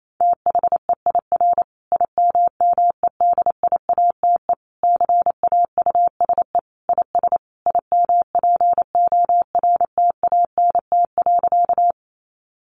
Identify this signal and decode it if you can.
Morse code, heard as a keyed tone, about 28 wpm, 700 Hz. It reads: THEIR IMMEDIATE CAUSE IS IMPORTANT.
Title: Sentences from Top 1200 Words 28wpm